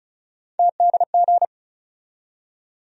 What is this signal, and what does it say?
Morse code, heard as a keyed tone, about 34 wpm, 700 Hz. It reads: TDG
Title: USA Call Sign Postfixes - 3 Letters 35wpm